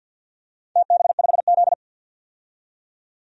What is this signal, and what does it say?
Morse code, heard as a keyed tone, about 50 wpm, 700 Hz. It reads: TB5Z
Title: DXCC-100 Call Signs - 4 Characters - 50wpm